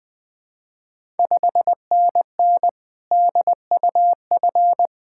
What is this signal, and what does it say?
Morse code, heard as a keyed tone, about 20 wpm, 700 Hz. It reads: 5NN DUF
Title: Ontario QSO Party Exchange 20wpm